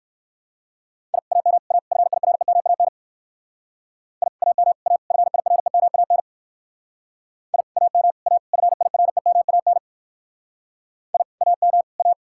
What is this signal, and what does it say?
Morse code, heard as a keyed tone, about 58 wpm, 700 Hz. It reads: I AM A FIREMAN I AM A FIREMAN I AM A FIREMAN I AM A